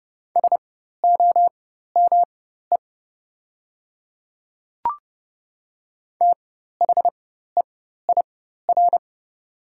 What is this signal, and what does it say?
Morse code, heard as a keyed tone, about 30 wpm, spaced as for 14 wpm, 700 Hz. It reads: SOME THEIR
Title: 4x Character Spacing - Top 200 Words - 30wpm